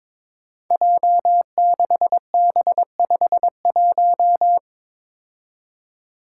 Morse code, heard as a keyed tone, about 22 wpm, 700 Hz. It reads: J6B51
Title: Five Challenging Letter-Number Combinations 22wpm